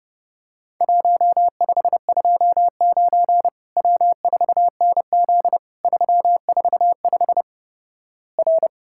Morse code, heard as a keyed tone, about 32 wpm, 700 Hz. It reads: 1529 W4NZ 345 R
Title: Worked All Europe DX Contest - QTC Exchange - 30wpm